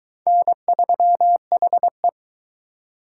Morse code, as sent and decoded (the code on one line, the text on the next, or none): -. ...-- .... .
N3HE